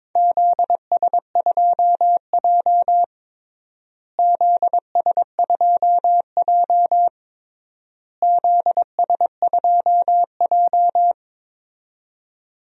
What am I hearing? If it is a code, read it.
Morse code, heard as a keyed tone, about 21 wpm, 700 Hz. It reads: ZS2J ZS2J ZS2J
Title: DXCC-100 Call Signs - 4 Characters - Repeated 3x - 22wpm